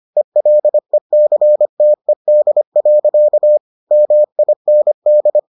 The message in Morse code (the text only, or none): ELECTED. MIND